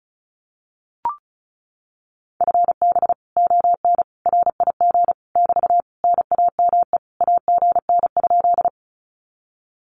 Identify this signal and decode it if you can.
Morse code, heard as a keyed tone, about 36 wpm, 700 Hz. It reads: FB ON RIG <BT> NAME AGN?
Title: Common QSO Phrases Encore 35wpm